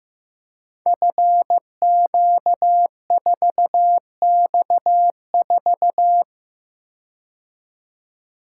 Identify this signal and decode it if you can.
Morse code, heard as a keyed tone, about 15 wpm, 700 Hz. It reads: FQ4X4
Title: Code Groups - 5 Letter:Numbers - 15wpm